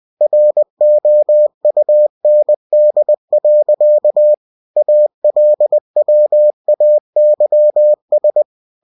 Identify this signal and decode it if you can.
Morse code, heard as a keyed tone, about 20 wpm, 600 Hz. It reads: ROUND. ALWAYS
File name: cw-books / winnie-the-pooh / winnie-20wpm-page-0109